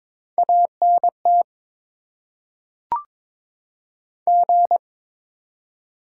Morse code, heard as a keyed tone, about 22 wpm, 700 Hz. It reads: ANT G
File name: Top 20 QSO Elements - 8x Character Spacing - 22wpm